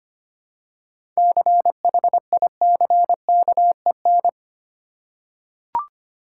Morse code, heard as a keyed tone, about 25 wpm, 690 Hz. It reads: CHICKEN
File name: practice / 25wpm / Easy Words with Hints - 25wpm